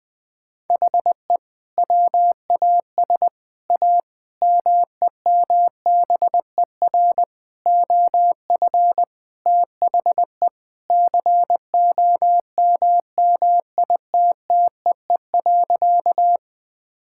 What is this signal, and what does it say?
Morse code, heard as a keyed tone, about 20 wpm, 700 Hz. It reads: HE WAS A MEMBER OF THE COMMITTEE.
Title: Sentences with 6-Word Phrases - 20wpm